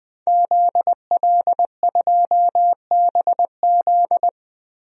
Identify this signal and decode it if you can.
Morse code, heard as a keyed tone, about 20 wpm, 700 Hz. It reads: ZL2BZ
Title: DXCC-100 Call Signs - 5 Characters - Repeated 3x - 20wpm